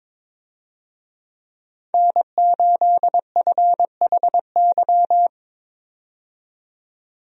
Morse code, heard as a keyed tone, about 22 wpm, 700 Hz. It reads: N8FHY
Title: USA Call Signs - 5 Characters - 22wpm